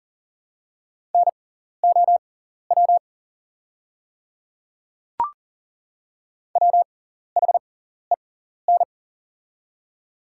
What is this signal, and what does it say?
Morse code, heard as a keyed tone, about 41 wpm, spaced as for 14 wpm, 700 Hz. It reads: NOW WHEN
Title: 6x Character Spacing - Top 100 Words - 40wpm